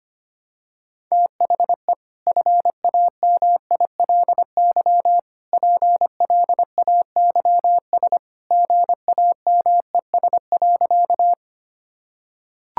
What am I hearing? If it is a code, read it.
Morse code, heard as a keyed tone, about 25 wpm, 700 Hz. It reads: THE FAMILY PLAYS GAMES.
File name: Easy Sentences with Hints - 25wpm